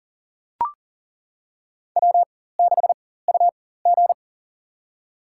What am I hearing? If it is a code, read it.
Morse code, heard as a keyed tone, about 40 wpm, spaced as for 19 wpm, 700 Hz. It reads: W6UG
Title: USA Call Signs - 4x Character Spacing - 40wpm